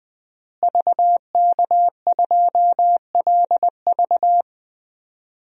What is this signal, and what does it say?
Morse code, heard as a keyed tone, about 20 wpm, 700 Hz. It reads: VK2LV
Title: DXCC-100 Call Signs - 5 Characters - Repeated 2x - 20wpm